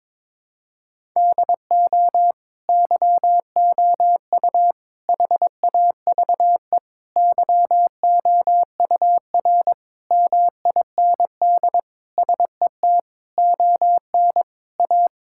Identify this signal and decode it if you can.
Morse code, heard as a keyed tone, about 22 wpm, 700 Hz. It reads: DO YOU HAVE YOUR MIND SET ON A
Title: Sentences from Top 1000 Words - Review Part 5 - 22wpm